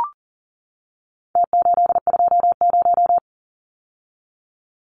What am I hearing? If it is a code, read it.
Morse code, heard as a keyed tone, about 40 wpm, 700 Hz. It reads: T820
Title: Numbers - Contest Style - 4-Digit 40wpm